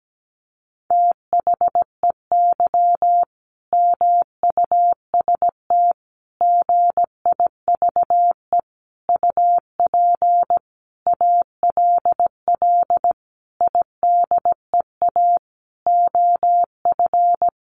Morse code, heard as a keyed tone, about 17 wpm, 700 Hz. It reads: THEY MUST GIVE UP ALL IDEA OF
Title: Sentences from Top 1100 Words 17wpm